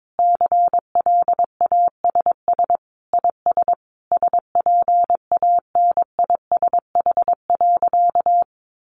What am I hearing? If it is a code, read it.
Morse code, heard as a keyed tone, about 22 wpm, 700 Hz. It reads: CLASS IS SPANISH.
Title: Additional Easy Phrases 22wpm